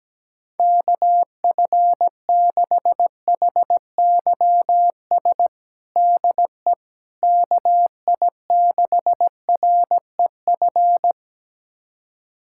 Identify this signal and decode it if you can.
Morse code, heard as a keyed tone, about 17 wpm, 700 Hz. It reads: KF6HYS DE KI6REF
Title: USA Callsign DE Callsign - 6 Characters - 17wpm